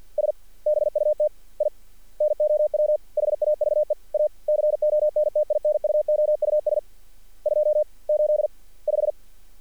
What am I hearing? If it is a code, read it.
Morse code, heard as a keyed tone, about 48 wpm, 610 Hz. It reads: S BUT I NOW HAVE A CONTINUOUS 285
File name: archived / CRSnet